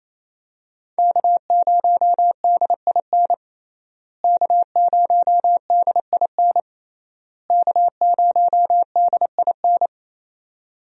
Morse code, heard as a keyed tone, about 27 wpm, 700 Hz. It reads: K0DIN K0DIN K0DIN
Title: USA Call Signs - 5 Characters - Repeated 3x - 28wpm